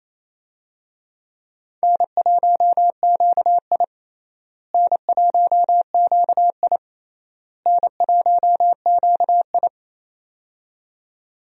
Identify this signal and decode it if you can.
Morse code, heard as a keyed tone, about 28 wpm, 700 Hz. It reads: N1QI N1QI N1QI
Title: USA Call Signs - 4 Characters - Repeated 3x - 28wpm